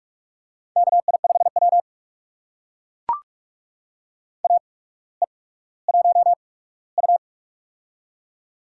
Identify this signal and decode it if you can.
Morse code, heard as a keyed tone, about 42 wpm, spaced as for 19 wpm, 700 Hz. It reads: KI5W AE1U
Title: USA Call Signs - 4 Characters - 8x Character Spacing - 45wpm